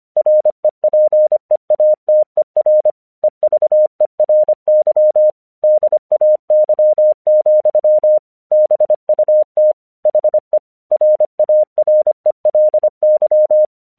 Morse code, heard as a keyed tone, about 26 wpm, 610 Hz. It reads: REPEATER EVERY DAY, BUT HE RARELY
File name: 1 English 25wpm / AQC14